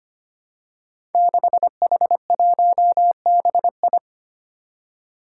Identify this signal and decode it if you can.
Morse code, heard as a keyed tone, about 25 wpm, 700 Hz. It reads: 6H1BI
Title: Five Challenging Letter-Number Combinations 25wpm